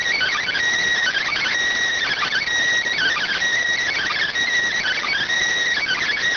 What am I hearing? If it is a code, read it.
Emergency Alert System (SAME) tones (header bursts), heard as harsh buzzing bursts. It is heard from 1.0 to 5.2 s.